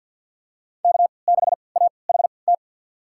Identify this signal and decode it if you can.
Morse code, heard as a keyed tone, about 51 wpm, spaced as for 28 wpm, 700 Hz. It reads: K6AHT
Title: USA Call Signs - 5 Characters - 3x Character Spacing - 50wpm